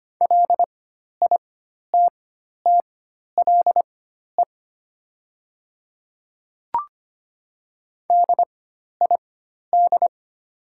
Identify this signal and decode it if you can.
Morse code, heard as a keyed tone, about 25 wpm, spaced as for 12 wpm, 700 Hz. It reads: LITTLE DID
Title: 4x Character Spacing - Top 500 Words - 25wpm